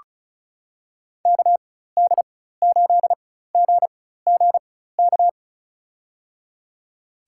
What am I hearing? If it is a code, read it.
Morse code, heard as a keyed tone, about 34 wpm, spaced as for 16 wpm, 700 Hz. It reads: KD8GGK
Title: USA Call Signs - 6 Characters - 4x Character Spacing - 35wpm